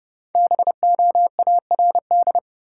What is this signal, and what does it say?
Morse code, heard as a keyed tone, about 30 wpm, 700 Hz. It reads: BOARD
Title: Sentences from Top 700 Words - New Word Drill - 30wpm